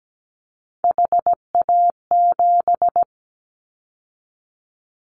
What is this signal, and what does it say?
Morse code, heard as a keyed tone, about 17 wpm, 700 Hz. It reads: HA7
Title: Callsign Prefixes for Top 25 DXCC - 17wpm